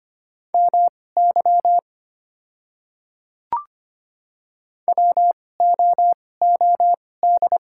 Morse code, heard as a keyed tone, about 25 wpm, spaced as for 18 wpm, 690 Hz. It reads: MY WOOD
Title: 2x Character Spacing - Top 300 Words - 25wpm